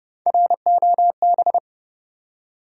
Morse code, heard as a keyed tone, about 32 wpm, 700 Hz. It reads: ROB
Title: USA Common Names - 30wpm